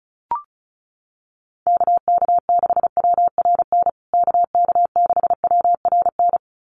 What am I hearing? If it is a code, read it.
Morse code, heard as a keyed tone, about 34 wpm, 700 Hz. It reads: KK6WRN KK6WRN
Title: USA Call Signs Repeated 2x 35wpm